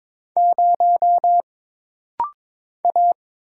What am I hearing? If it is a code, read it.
Morse code, heard as a keyed tone, about 22 wpm, 700 Hz. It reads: 0 A
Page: This rapid-fire timing is geared to push your ICR (Instant Character Recognition) skills.